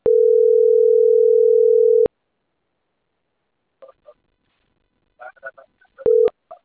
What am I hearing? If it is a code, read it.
Ringback tone (North America), 440 + 480 Hz.